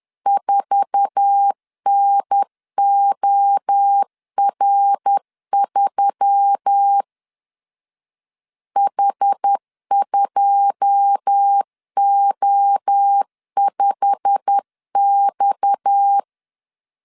Morse code, heard as a keyed tone, about 10 wpm, 800 Hz. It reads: 4NOR3 H2O5X
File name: CW / DET / 1 9 1500C S A 10wpm 4511